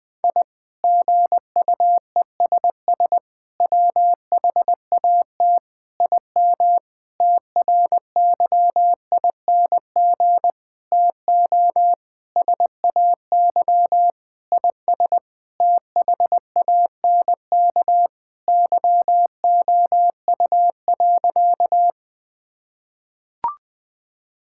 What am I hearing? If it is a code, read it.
Morse code, heard as a keyed tone, about 20 wpm, 700 Hz. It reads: I GUESS WHAT IM TRYING TO SAY IS THANK YOU.
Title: Sentences with 8-Word Phrases - Spoken Beginning - 20wpm